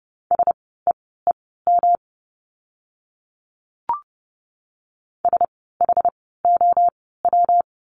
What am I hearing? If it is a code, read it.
Morse code, heard as a keyed tone, about 30 wpm, spaced as for 17 wpm, 700 Hz. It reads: SEEM SHOW
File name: course / 30:10wpm / 115 Lesson 19f - Letter w - Words 30wpm